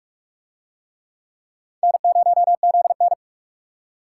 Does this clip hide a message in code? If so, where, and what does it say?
Yes: Morse code from 1.8 to 3.1 s, heard as a keyed tone, about 44 wpm, 700 Hz. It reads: N0ZN